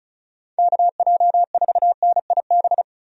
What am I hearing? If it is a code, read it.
Morse code, heard as a keyed tone, about 34 wpm, 700 Hz. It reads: KJ4NIB
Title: USA Call Signs - 6 Characters - Repeated 3x - 35wpm